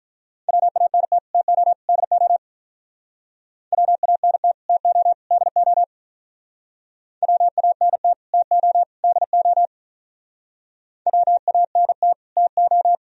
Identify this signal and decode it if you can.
Morse code, heard as a keyed tone, about 43 wpm, 700 Hz. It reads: WANT TO DO WANT TO DO WANT TO DO WANT TO
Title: Speed-Racing - Sets of 3 Words 35wpm